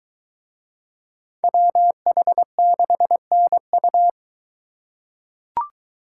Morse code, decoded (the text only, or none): WH6NU